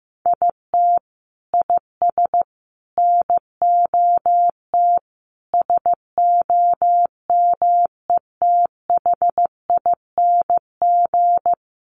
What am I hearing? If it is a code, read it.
Morse code, heard as a keyed tone, about 15 wpm, 700 Hz. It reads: IT IS NOT SOMETHING
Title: Sentences from Top 1400 Words 15wpm